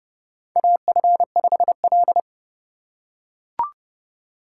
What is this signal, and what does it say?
Morse code, heard as a keyed tone, about 30 wpm, 700 Hz. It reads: AF5L
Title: USA Call Signs - 4 Characters - 2x Character Spacing - 30wpm